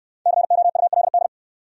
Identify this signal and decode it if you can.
Morse code, heard as a keyed tone, about 68 wpm, 700 Hz. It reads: COULD